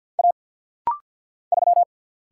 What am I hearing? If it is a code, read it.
Morse code, heard as a keyed tone, about 48 wpm, 700 Hz. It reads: A 3
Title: Single Letter-Number - Rapid-Fire 50wpm